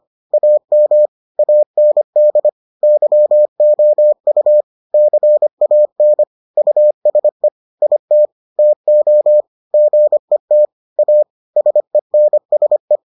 Morse code, heard as a keyed tone, about 25 wpm, 600 Hz. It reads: AM AND YOU CAN USE IT TO GET A SENSE